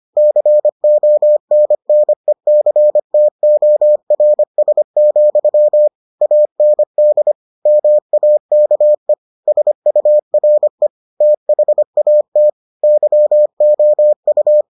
Morse code, heard as a keyed tone, about 25 wpm, 600 Hz. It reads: CONNECTORS, AND MAKE SURE THAT YOU